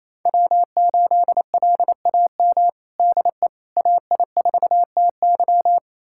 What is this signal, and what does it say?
Morse code, heard as a keyed tone, about 28 wpm, 700 Hz. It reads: W8LAM DE AI4TY
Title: USA Callsign DE Callsign - 5 Characters - 28wpm